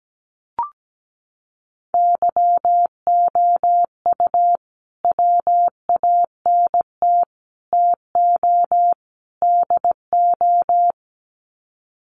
Morse code, heard as a keyed tone, about 17 wpm, 700 Hz. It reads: YOU WANT TO DO
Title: Sets of 4 Words 17wpm